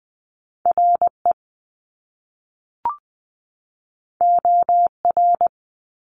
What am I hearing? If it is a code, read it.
Morse code, heard as a keyed tone, about 20 wpm, 700 Hz. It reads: RE OR
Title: Top Two Letter Combinations in Words 20wpm